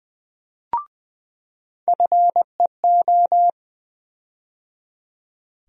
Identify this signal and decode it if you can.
Morse code, heard as a keyed tone, about 20 wpm, 700 Hz. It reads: FEO
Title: USA Call Sign Postfixes - 3 Letters 20wpm